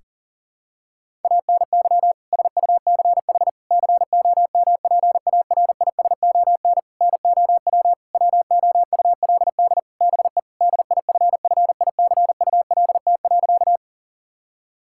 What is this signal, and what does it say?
Morse code, heard as a keyed tone, about 38 wpm, 700 Hz. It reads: ANY SUCH COMPARISON NOW WOULD BE DIFFICULT.
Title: Sentences from Top 1600 Words 40wpm